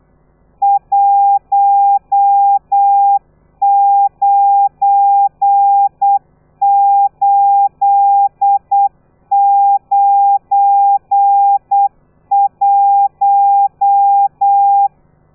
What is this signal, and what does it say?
Morse code, heard as a keyed tone, about 8 wpm, 800 Hz. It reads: 19891